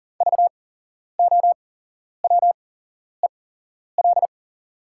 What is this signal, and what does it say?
Morse code, heard as a keyed tone, about 40 wpm, spaced as for 11 wpm, 690 Hz. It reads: VOWEL
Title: 8x Character Spacing - Top 500 Words - 40wpm